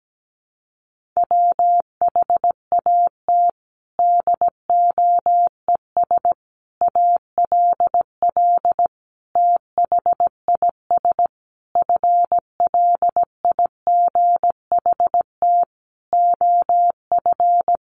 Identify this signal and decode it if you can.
Morse code, heard as a keyed tone, about 17 wpm, 700 Hz. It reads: WHAT DOES ALL THIS FLIGHT OF
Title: Sentences from Top 1300 Words 17wpm